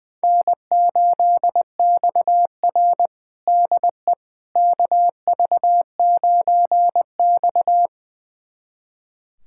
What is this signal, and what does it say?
Morse code, heard as a keyed tone, about 20 wpm, 700 Hz. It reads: N8XR DE KV9X
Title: USA Callsign DE Callsign - 4 Characters - 20wpm